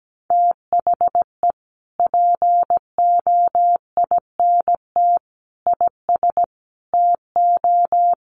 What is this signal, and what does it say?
Morse code, heard as a keyed tone, about 17 wpm, 700 Hz. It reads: THE POINT IS TO
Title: Sentences from Top 1300 Words 17wpm